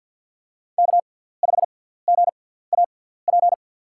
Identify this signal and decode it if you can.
Morse code, heard as a keyed tone, about 52 wpm, spaced as for 17 wpm, 700 Hz. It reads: K5GAP
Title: USA Call Signs - 5 Characters - 6x Character Spacing - 50wpm